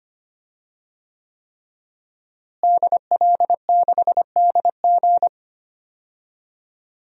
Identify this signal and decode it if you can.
Morse code, heard as a keyed tone, about 25 wpm, 700 Hz. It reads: DL6DG
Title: DXCC-100 Call Signs - 5 Characters - Repeated 3x - 25wpm